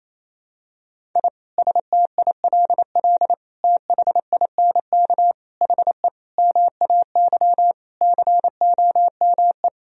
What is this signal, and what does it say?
Morse code, heard as a keyed tone, about 28 wpm, 700 Hz. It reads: I STILL THINK HE MAY COME
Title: Sentences from Top 1000 Words - Review Part 1 - 28wpm